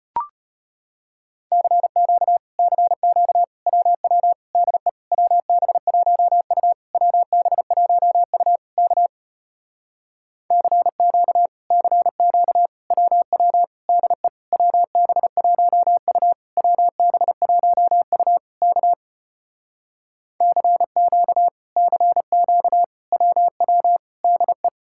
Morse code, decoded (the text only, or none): CQ CQ WW DE WB1U WB1U K CQ CQ WW DE WB1U WB1U K CQ CQ WW DE